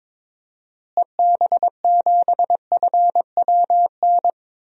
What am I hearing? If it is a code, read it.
Morse code, heard as a keyed tone, about 22 wpm, 700 Hz. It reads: EB7FWN
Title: DXCC-100 Call Signs - 6 Characters - Repeated 2x - 22wpm